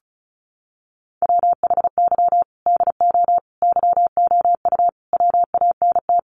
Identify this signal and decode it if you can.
Morse code, heard as a keyed tone, about 34 wpm, 700 Hz. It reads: WHY DO YOU WANT
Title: Sentences from Top 200 Words 35wpm